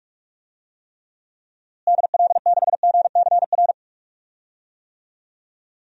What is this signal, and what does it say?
Morse code, heard as a keyed tone, about 44 wpm, 700 Hz. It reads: DL6GCR
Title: DXCC-100 Call Signs - 6 Characters - Repeated 3x - 45wpm